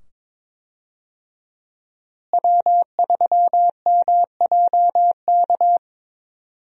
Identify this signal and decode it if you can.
Morse code, heard as a keyed tone, about 22 wpm, 700 Hz. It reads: W3MJK
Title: USA Call Signs - 5 Characters - Repeated 2x - 22wpm